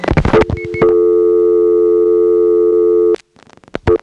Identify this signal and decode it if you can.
Dial tone (North America), 350 + 440 Hz.